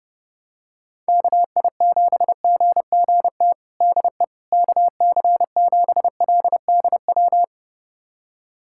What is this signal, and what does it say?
Morse code, heard as a keyed tone, about 30 wpm, 700 Hz. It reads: KI7GGT DE KC7LDW
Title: USA Callsign DE Callsign - 6 Characters - 30wpm